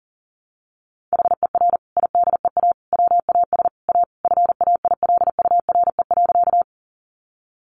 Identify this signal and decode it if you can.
Morse code, heard as a keyed tone, about 40 wpm, 700 Hz. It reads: HER IDEA WAS A FAILURE.
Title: Sentences from Top 900 Words 40wpm